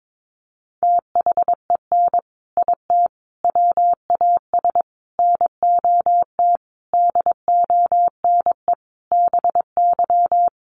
Morse code, heard as a keyed tone, about 22 wpm, 700 Hz. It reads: THEN IT WAS NOT DONE BY
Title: Sentences from Top 1300 Words 22wpm